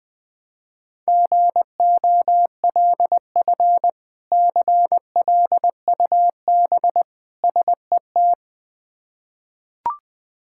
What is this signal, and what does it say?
Morse code, heard as a keyed tone, about 21 wpm, 700 Hz. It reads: GOLF CLUB SET
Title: Difficult Words with Hints - 20wpm